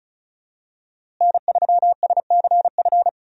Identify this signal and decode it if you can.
Morse code, heard as a keyed tone, about 36 wpm, 700 Hz. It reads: N3SCF
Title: USA Call Signs - 5 Characters - Repeated 2x - 35wpm